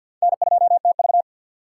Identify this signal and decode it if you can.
Morse code, heard as a keyed tone, about 52 wpm, 700 Hz. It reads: N2TV
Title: USA Call Signs - 4 Characters - 2x Character Spacing - 50wpm